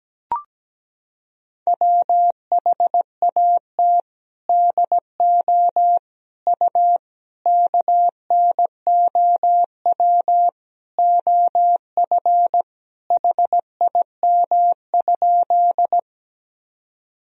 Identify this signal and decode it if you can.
Morse code, heard as a keyed tone, about 17 wpm, 700 Hz. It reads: WHAT DO U KNOW OF HIM?
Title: Sentences from Top 100 Words and Abbreviations 17wpm